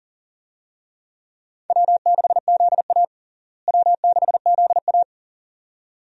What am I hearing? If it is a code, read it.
Morse code, heard as a keyed tone, about 40 wpm, 700 Hz. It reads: W6ZA W6ZA
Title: USA Call Signs - 4 Characters - Repeated 2x - 40wpm